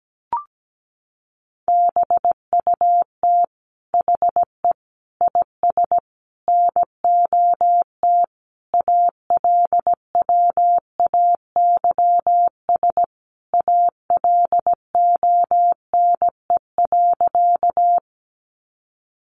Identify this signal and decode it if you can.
Morse code, heard as a keyed tone, about 17 wpm, 700 Hz. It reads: BUT HE IS NOT ALWAYS ALONE.
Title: Sentences from Top 1000 Words 17wpm